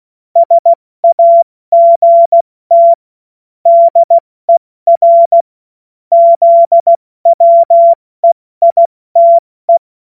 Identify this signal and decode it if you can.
Morse code, heard as a keyed tone, about 16 wpm, 680 Hz. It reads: SAGT DER ZWEITE
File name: Morsekurs / Witze / Witz 01. Februar 2022 70 F80